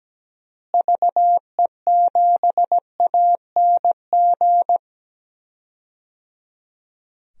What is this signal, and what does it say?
Morse code, heard as a keyed tone, about 17 wpm, 700 Hz. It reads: VE7ANG
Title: DXCC-100 Call Signs - 6 Characters - Repeated 3x - 17wpm